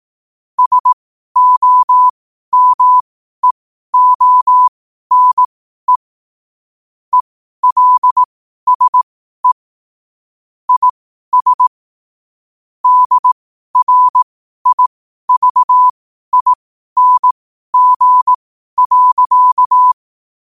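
Morse code, heard as a keyed tone, about 18 wpm, spaced as for 12 wpm, 1000 Hz. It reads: SOMEONE ELSE IS DRIVING.
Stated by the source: Morse code sent at 12 words per minute.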